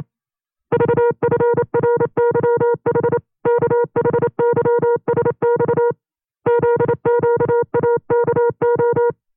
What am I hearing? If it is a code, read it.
Morse code, heard as a keyed tone, about 28 wpm, 480 Hz. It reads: VFRYH KHYSX ZQAKO
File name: FreeFiles / CW-practice / 25WPM-side1